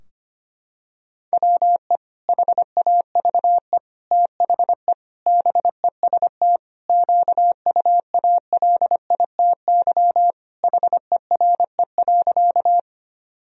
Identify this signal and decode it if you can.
Morse code, heard as a keyed tone, about 25 wpm, 700 Hz. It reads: WE HAVE THE BEST QUALITY HERE.
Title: Sentences from Top 1000 Words - Review Part 4 - 25wpm